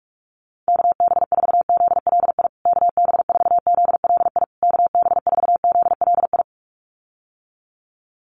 Morse code, heard as a keyed tone, about 44 wpm, 700 Hz. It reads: KB4ZLI KB4ZLI KB4ZLI
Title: USA Call Signs Repeated 3x 45wpm